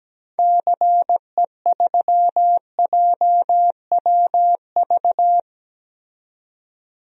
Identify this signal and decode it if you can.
Morse code, heard as a keyed tone, about 17 wpm, 700 Hz. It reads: CE3JWV
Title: DXCC-100 Call Signs - 6 Characters - 17wpm